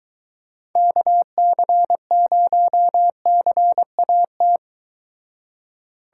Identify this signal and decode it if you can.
Morse code, heard as a keyed tone, about 23 wpm, 700 Hz. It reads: KC0CAT